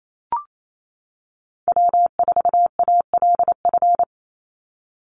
Morse code, heard as a keyed tone, about 28 wpm, 700 Hz. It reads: W4ALF
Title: Call Signs - 28wpm